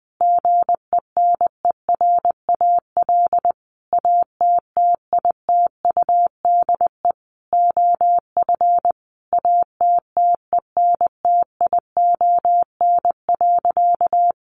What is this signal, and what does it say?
Morse code, heard as a keyed tone, about 20 wpm, 700 Hz. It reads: GENERAL ATTITUDE OF ATTENTION.
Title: Sentences from Top 1300 Words 20wpm